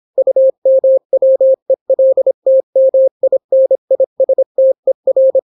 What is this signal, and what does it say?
Morse code, heard as a keyed tone, about 26 wpm, 530 Hz. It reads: UMWELTMINISTER